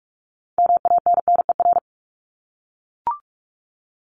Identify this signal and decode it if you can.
Morse code, heard as a keyed tone, about 44 wpm, 700 Hz. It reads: MANNER
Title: Top 1501-1750 Words 45wpm